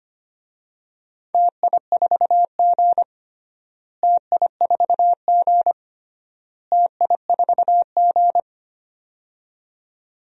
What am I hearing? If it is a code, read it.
Morse code, heard as a keyed tone, about 25 wpm, 700 Hz. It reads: TI4G TI4G TI4G
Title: DXCC-100 Call Signs - 4 Characters - Repeated 3x - 25wpm